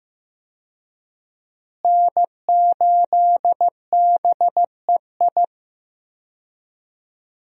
Morse code, heard as a keyed tone, about 16 wpm, 700 Hz. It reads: N8BEI
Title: USA Call Signs - 5 Characters - Repeated 3x - 15wpm